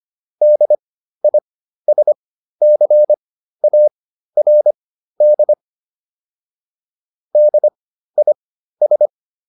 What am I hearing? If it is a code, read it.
Morse code, heard as a keyed tone, about 25 wpm, spaced as for 13 wpm, 600 Hz. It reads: DISCARD DIS